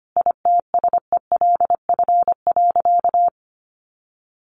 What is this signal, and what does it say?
Morse code, heard as a keyed tone, about 25 wpm, 700 Hz. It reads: ITSELF.
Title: Sentences from Top 1500 Words 25wpm